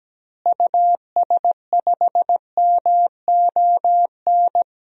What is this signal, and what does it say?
Morse code, heard as a keyed tone, about 17 wpm, 700 Hz. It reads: US5MON
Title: DXCC-100 Call Signs - 6 Characters - Repeated 3x - 17wpm